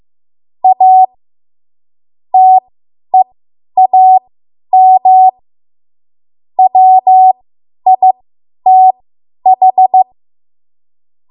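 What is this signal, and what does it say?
Morse code, heard as a keyed tone, about 15 wpm, spaced as for 10 wpm, 750 Hz. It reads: A TEAM WITH
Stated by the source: Morse code practice run at 10 WPM from a code practice archive.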